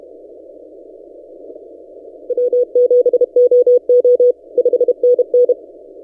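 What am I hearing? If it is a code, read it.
Morse code, heard as a keyed tone, about 31 wpm, 490 Hz. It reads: W7OO 5NN
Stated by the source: Morse from DXpedition station A35X on 12 m, posted by N6ML.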